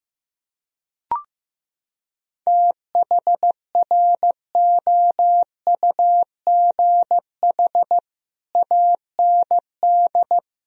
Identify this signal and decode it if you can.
Morse code, heard as a keyed tone, about 16 wpm, 700 Hz. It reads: THROUGH AND
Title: Sentences from Top 1000 Words - Review Part 1 - 15wpm